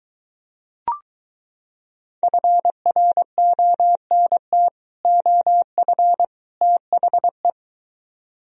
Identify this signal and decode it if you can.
Morse code, heard as a keyed tone, about 24 wpm, 700 Hz. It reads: FRONT OF THE